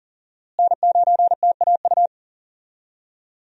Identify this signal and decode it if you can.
Morse code, heard as a keyed tone, about 40 wpm, 700 Hz. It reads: N9TAU
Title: USA Call Signs - 5 Characters - Repeated 2x - 40wpm